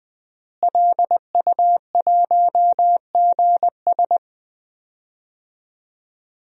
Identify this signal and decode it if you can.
Morse code, heard as a keyed tone, about 20 wpm, 700 Hz. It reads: LU1GS
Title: DXCC-100 Call Signs - 5 Characters - Repeated 3x - 20wpm